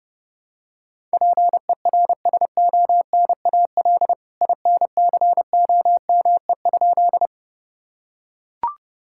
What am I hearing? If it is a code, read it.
Morse code, heard as a keyed tone, about 32 wpm, 700 Hz. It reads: PERSONAL INCOME?
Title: Sentences from Top 1000 Words - Review Part 5 - 30wpm